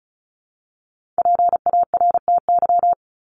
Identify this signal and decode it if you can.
Morse code, heard as a keyed tone, about 34 wpm, 700 Hz. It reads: PARTY
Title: ES:S Suffix - Top 100 Words with Base Words 35wpm